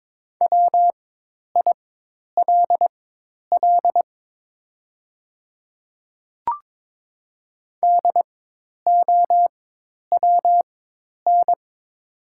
Morse code, heard as a keyed tone, about 22 wpm, spaced as for 10 wpm, 700 Hz. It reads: WILL DOWN
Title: 4x Character Spacing - Top 300 Words - 22wpm